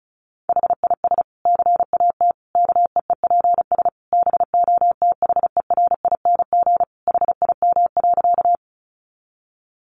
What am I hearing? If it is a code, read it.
Morse code, heard as a keyed tone, about 34 wpm, 700 Hz. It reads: HIS CAT KEEPS BOTHERING HIM.
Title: Additional Easy Phrases 35wpm